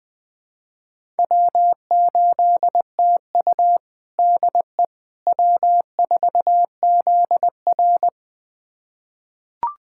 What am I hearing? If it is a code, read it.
Morse code, heard as a keyed tone, about 20 wpm, 700 Hz. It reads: W8TU DE W4ZR
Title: USA Callsign DE Callsign - 4 Characters - 20wpm